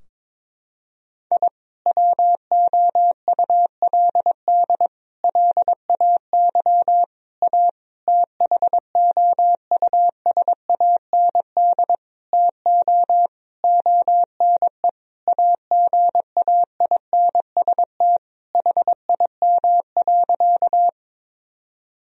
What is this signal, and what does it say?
Morse code, heard as a keyed tone, about 22 wpm, 700 Hz. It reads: I WOULD LAY A THOUSAND TO ONE AGAINST HIM.
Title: Sentences from Top 1000 Words - Review Part 3 - 22wpm